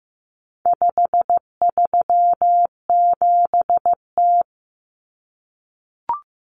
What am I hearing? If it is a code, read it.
Morse code, heard as a keyed tone, about 15 wpm, 700 Hz. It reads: 537T
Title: Common Cut Numbers - 4-Digit 15wpm